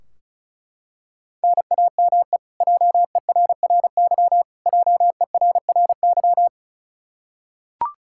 Morse code, heard as a keyed tone, about 36 wpm, 700 Hz. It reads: NAME JERRY JERRY
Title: Practice QSO - Exchange 1 - Beginning - 35wpm